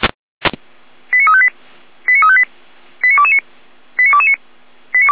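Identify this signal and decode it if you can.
Five-tone selective call (ZVEI), heard as quick runs of five tones, tones of about 70 ms. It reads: 89358 89358 89229 89229
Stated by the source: ZVEI five-tone selective call received on SDR.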